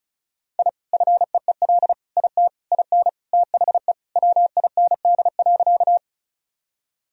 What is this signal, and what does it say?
Morse code, heard as a keyed tone, about 34 wpm, 700 Hz. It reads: I FEEL IT IN THE WIND.
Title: Sentences from Top 1000 Words - Review Part 2 - 35wpm